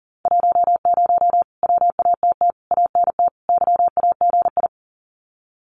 Morse code, heard as a keyed tone, about 40 wpm, 700 Hz. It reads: 10 WATT ANT YAGI
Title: Common QSO Phrases Encore 40wpm